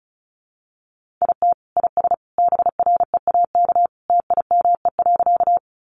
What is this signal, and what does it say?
Morse code, heard as a keyed tone, about 34 wpm, 700 Hz. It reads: IT IS BREAK TIME.
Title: Easy Phrases 35wpm